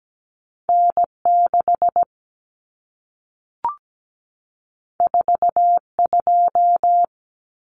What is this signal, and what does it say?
Morse code, heard as a keyed tone, about 17 wpm, 700 Hz. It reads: N6 42
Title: Common Cut Numbers - 2-Digit 17wpm